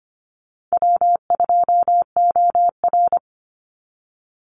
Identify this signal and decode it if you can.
Morse code, heard as a keyed tone, about 25 wpm, 700 Hz. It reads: W2OR
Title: Easy Call Signs Repeated 3x - 25wpm